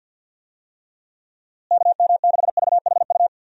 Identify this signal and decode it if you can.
Morse code, heard as a keyed tone, about 52 wpm, 700 Hz. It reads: KM6VHU
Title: USA Call Signs - 6 Characters - Repeated 3x - 50wpm